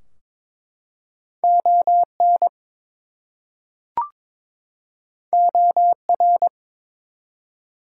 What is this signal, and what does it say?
Morse code, heard as a keyed tone, about 22 wpm, 700 Hz. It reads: ON OR
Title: Two Letter Words - 22wpm